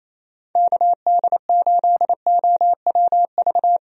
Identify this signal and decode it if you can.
Morse code, heard as a keyed tone, about 28 wpm, 700 Hz. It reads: KD8OWV
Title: USA Call Signs - 6 Characters - Repeated 2x - 28wpm